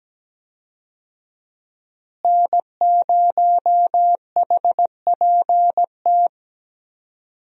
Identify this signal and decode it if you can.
Morse code, heard as a keyed tone, about 17 wpm, 700 Hz. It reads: N0HPT
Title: USA Call Signs - 5 Characters - Repeated 3x - 17wpm